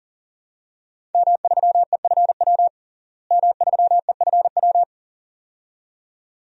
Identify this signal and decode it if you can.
Morse code, heard as a keyed tone, about 40 wpm, 690 Hz. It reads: M3EFW M3EFW
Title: DXCC-100 Call Signs - 5 Characters - Repeated 2x - 40wpm